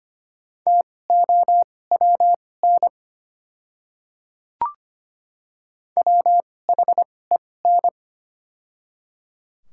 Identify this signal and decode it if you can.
Morse code, heard as a keyed tone, about 25 wpm, spaced as for 18 wpm, 700 Hz. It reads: TOWN WHEN
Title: 2x Character Spacing - Top 500 Words - 25wpm